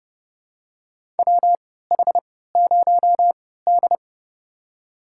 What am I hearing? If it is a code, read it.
Morse code, heard as a keyed tone, about 30 wpm, spaced as for 17 wpm, 700 Hz. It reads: WH0D
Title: USA Call Signs - 4 Characters - 3x Character Spacing - 30wpm